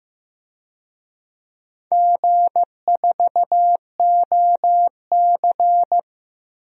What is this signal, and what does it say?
Morse code, heard as a keyed tone, about 15 wpm, 700 Hz. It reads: G4OC
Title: DXCC-100 Call Signs - 4 Characters - Repeated 3x - 15wpm